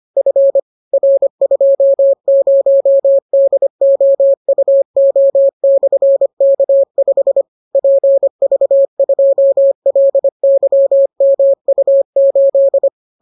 Morse code, heard as a keyed tone, about 25 wpm, 550 Hz. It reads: F R20DOUO/K5 PV2LYMU8